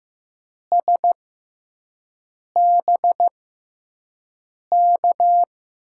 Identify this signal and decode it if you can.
Morse code, heard as a keyed tone, about 15 wpm, spaced as for 5 wpm, 700 Hz. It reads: SBK
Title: USA Call Signs - 6 Characters - 6x Character Spacing - 15wpm